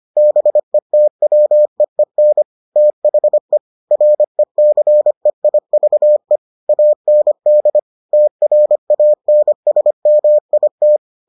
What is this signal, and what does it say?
Morse code, heard as a keyed tone, about 25 wpm, 600 Hz. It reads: BETWEEN THE RECEIVE AND TRANSMIT